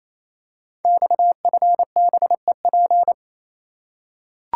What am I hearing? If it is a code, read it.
Morse code, heard as a keyed tone, about 27 wpm, 700 Hz. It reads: XFBEP
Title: Code Groups - 5 Letter:Numbers - 28wpm